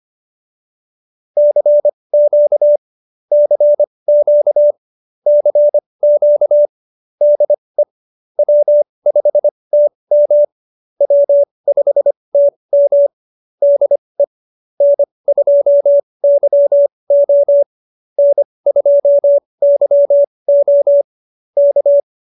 Morse code, heard as a keyed tone, about 25 wpm, spaced as for 20 wpm, 590 Hz. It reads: CQ CQ CQ DE W5TM W5TM DE N2YO N2YO K
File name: QSO316 20